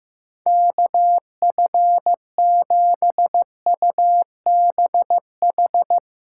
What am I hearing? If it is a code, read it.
Morse code, heard as a keyed tone, about 15 wpm, 700 Hz. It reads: KF7UBH
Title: USA Callsign DE Callsign - 6 Characters - 15wpm